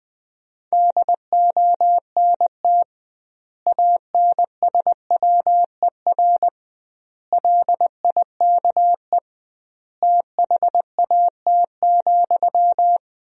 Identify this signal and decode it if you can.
Morse code, heard as a keyed tone, about 21 wpm, 700 Hz. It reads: DONT ANSWER LIKE THAT,
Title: 2x Word-Spacing - Sentences from Top 400 Words - 20wpm